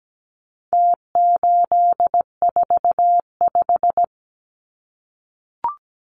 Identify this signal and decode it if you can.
Morse code, heard as a keyed tone, about 17 wpm, 700 Hz. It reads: T845
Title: Numbers - Contest Style - 4-Digit 17wpm